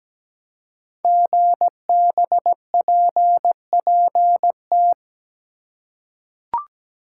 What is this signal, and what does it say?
Morse code, heard as a keyed tone, about 17 wpm, 700 Hz. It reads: GBPPT
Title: Code Groups - 5 Letters - 17wpm